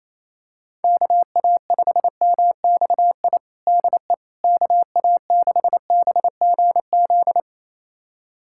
Morse code, heard as a keyed tone, about 28 wpm, 700 Hz. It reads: KA5MXI DE KA6BGZ
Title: USA Callsign DE Callsign - 6 Characters - 28wpm